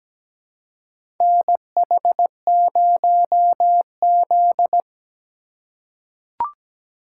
Morse code, heard as a keyed tone, about 17 wpm, 700 Hz. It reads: NH0Z
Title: USA Call Signs - 4 Characters - 2x Character Spacing - 17wpm